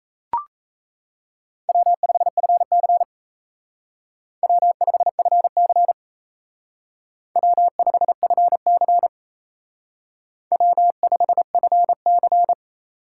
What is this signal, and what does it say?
Morse code, heard as a keyed tone, about 34 wpm, 700 Hz. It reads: W5FC W5FC W5FC W5FC
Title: Speed Racing - USA 1x2 Call Signs 28wpm